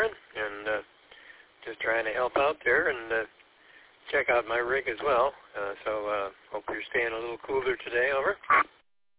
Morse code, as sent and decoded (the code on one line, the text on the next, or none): none